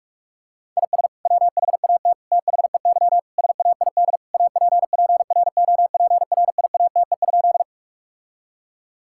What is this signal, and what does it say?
Morse code, heard as a keyed tone, about 46 wpm, 700 Hz. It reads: IS WHAT THEY SAID APPROPRIATE?
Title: Sentences from Top 1000 Words - Review Part 5 - 45wpm